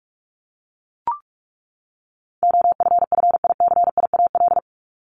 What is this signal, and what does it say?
Morse code, heard as a keyed tone, about 44 wpm, 700 Hz. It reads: OFFICIAL
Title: Top 751-1000 Words 45wpm